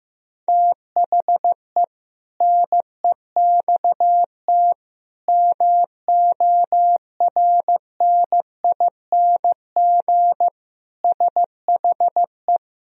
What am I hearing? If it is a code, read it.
Morse code, heard as a keyed tone, about 16 wpm, 700 Hz. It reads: THE NEXT MORNING SHE
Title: Sentences from Top 1000 Words - Review Part 2 - 15wpm